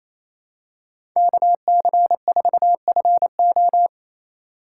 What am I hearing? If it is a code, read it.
Morse code, heard as a keyed tone, about 28 wpm, 700 Hz. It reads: KC4FO
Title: USA Call Signs - 5 Characters - 2x Character Spacing - 28wpm